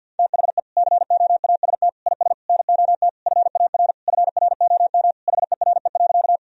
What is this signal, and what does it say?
Morse code, heard as a keyed tone, about 50 wpm, 700 Hz. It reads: THE COAST IS NOT FAR FROM HERE.
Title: Sentences from Top 1000 Words - Review Part 4 - 50wpm